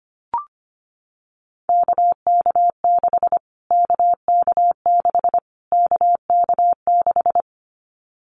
Morse code, heard as a keyed tone, about 25 wpm, 700 Hz. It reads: KK6 KK6 KK6
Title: Callsign Prefixes for Top 25 DXCC - Repeated 3x - 25wpm